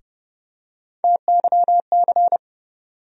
Morse code, heard as a keyed tone, about 30 wpm, 700 Hz. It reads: TYC
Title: USA Call Sign Postfixes - 3 Letters 30wpm